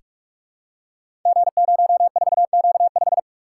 Morse code, heard as a keyed tone, about 44 wpm, 700 Hz. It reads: G04Q5
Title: Code Groups - 5 Letter:Numbers - 45wpm